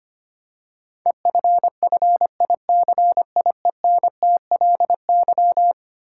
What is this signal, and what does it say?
Morse code, heard as a keyed tone, about 25 wpm, 700 Hz. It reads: EFFICIENTLY
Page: morse-code-ninja-practice-25wpm